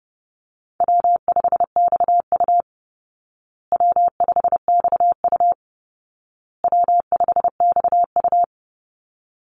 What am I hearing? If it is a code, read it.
Morse code, heard as a keyed tone, about 28 wpm, 700 Hz. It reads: W5XU W5XU W5XU
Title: Easy Call Signs Repeated 3x - 30wpm